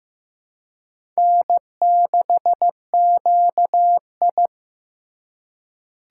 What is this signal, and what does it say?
Morse code, heard as a keyed tone, about 15 wpm, 700 Hz. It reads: N6QI
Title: USA Call Signs - 4 Characters - 3x Character Spacing - 15wpm